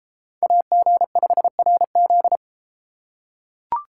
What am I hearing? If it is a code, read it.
Morse code, heard as a keyed tone, about 31 wpm, 700 Hz. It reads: AG5RZ